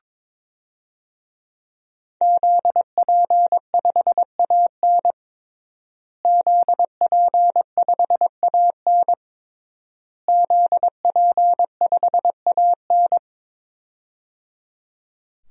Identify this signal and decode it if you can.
Morse code, heard as a keyed tone, about 22 wpm, 700 Hz. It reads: ZP5AN ZP5AN ZP5AN
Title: DXCC-100 Call Signs - 5 Characters - Repeated 3x - 22wpm